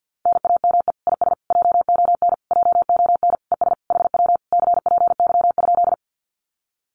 Morse code, heard as a keyed tone, about 50 wpm, 700 Hz. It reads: NAME IS JON JON ES HW CPY?
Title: Common QSO Phrases Encore 50wpm